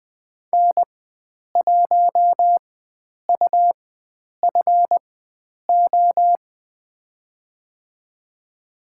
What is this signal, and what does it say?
Morse code, heard as a keyed tone, about 20 wpm, spaced as for 9 wpm, 700 Hz. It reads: N1UFO
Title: USA Call Signs - 5 Characters - 4x Character Spacing - 20wpm